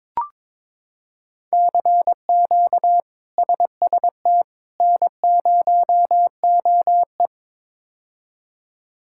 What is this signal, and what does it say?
Morse code, heard as a keyed tone, about 22 wpm, 700 Hz. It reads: CQ SST N0OE
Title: K1USN SST QSO Practice - 22wpm